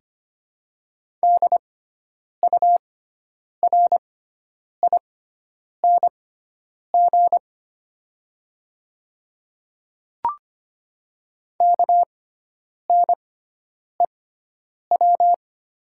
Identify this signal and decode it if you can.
Morse code, heard as a keyed tone, about 25 wpm, spaced as for 9 wpm, 700 Hz. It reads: DURING KNEW
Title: 6x Character Spacing - Top 500 Words - 25wpm